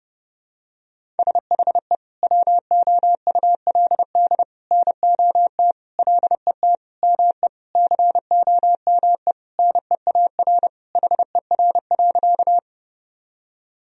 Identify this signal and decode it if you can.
Morse code, heard as a keyed tone, about 32 wpm, 700 Hz. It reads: SHE WOULD NOT LET ME COME NEAR HER.
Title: Sentences from Top 1000 Words - Review Part 1 - 30wpm